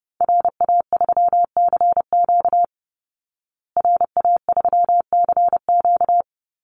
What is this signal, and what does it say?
Morse code, heard as a keyed tone, about 30 wpm, 700 Hz. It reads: RA3CQ RA3CQ
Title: Call Signs Repeated 3x - 30wpm